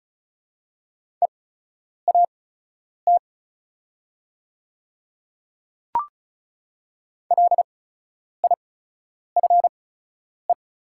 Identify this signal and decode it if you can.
Morse code, heard as a keyed tone, about 36 wpm, spaced as for 10 wpm, 700 Hz. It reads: EAT LIFE
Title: 8x Character Spacing - Top 300 Words - 35wpm